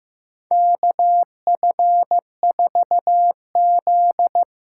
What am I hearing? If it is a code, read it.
Morse code, heard as a keyed tone, about 15 wpm, 700 Hz. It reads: KF4Z
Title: USA Call Signs - 6 Characters - 6x Character Spacing - 15wpm